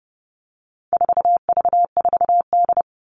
Morse code, heard as a keyed tone, about 30 wpm, 700 Hz. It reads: 4V4D
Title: All Cut Numbers - 4-Digit 30wpm